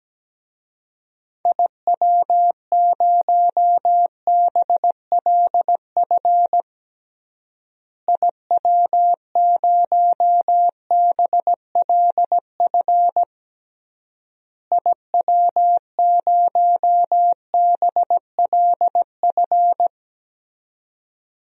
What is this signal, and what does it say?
Morse code, heard as a keyed tone, about 17 wpm, 700 Hz. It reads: IW0BLF IW0BLF IW0BLF
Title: DXCC-100 Call Signs - 6 Characters - Repeated 3x - 17wpm